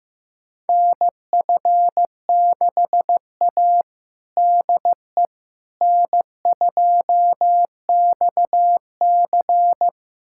Morse code, heard as a keyed tone, about 16 wpm, 700 Hz. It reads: NF6A DE N2XC
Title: USA Callsign DE Callsign - 4 Characters - 15wpm